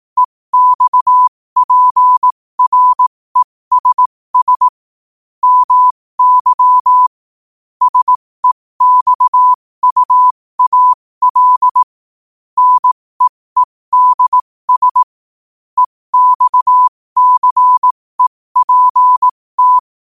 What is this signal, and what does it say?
Morse code, heard as a keyed tone, about 18 wpm, spaced as for 15 wpm, 1000 Hz. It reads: EXPRESS MY SEXUAL NEEDS EXCEPT
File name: qotdfiles / 15wpm